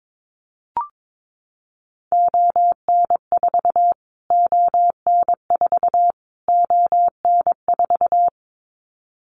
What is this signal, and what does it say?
Morse code, heard as a keyed tone, about 22 wpm, 700 Hz. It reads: ON4 ON4 ON4
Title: Callsign Prefixes for Top 25 DXCC - Repeated 3x - 22wpm